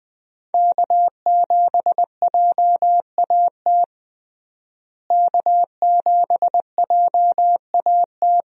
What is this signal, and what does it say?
Morse code, heard as a keyed tone, about 20 wpm, 700 Hz. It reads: K7JAT K7JAT
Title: USA Call Signs - 5 Characters - Repeated 2x - 20wpm